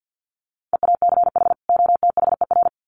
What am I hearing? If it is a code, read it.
Morse code, heard as a keyed tone, about 50 wpm, 700 Hz. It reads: EACH OTHER
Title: Sets of 2 Words 50wpm